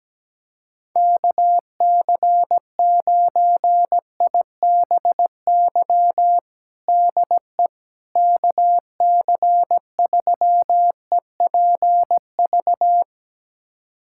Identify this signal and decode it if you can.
Morse code, heard as a keyed tone, about 17 wpm, 700 Hz. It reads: KC9IBY DE KC3EPV
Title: USA Callsign DE Callsign - 6 Characters - 17wpm